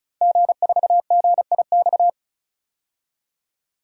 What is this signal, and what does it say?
Morse code, heard as a keyed tone, about 34 wpm, 700 Hz. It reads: G4GIX
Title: DXCC-100 Call Signs - 5 Characters - 35wpm